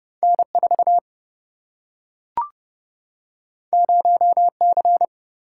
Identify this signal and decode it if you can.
Morse code, heard as a keyed tone, about 30 wpm, 700 Hz. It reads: N 4 0 C
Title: Single-Letter Single-Number 30wpm